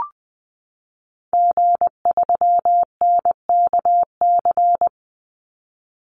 Morse code, heard as a keyed tone, about 19 wpm, 700 Hz. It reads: G3NKC
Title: Call Signs Repeated 2x - 20wpm